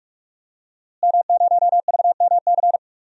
Morse code, heard as a keyed tone, about 46 wpm, 700 Hz. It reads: M0VMC
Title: DXCC-100 Call Signs - 5 Characters - Repeated 2x - 45wpm